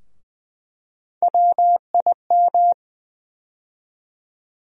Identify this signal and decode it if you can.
Morse code, heard as a keyed tone, about 20 wpm, 700 Hz. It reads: WIM
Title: USA Call Sign Postfixes - 3 Letters 20wpm